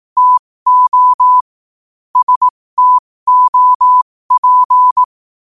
Morse code, heard as a keyed tone, about 18 wpm, spaced as for 15 wpm, 1000 Hz. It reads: TO STOP
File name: qotdfiles / 15wpm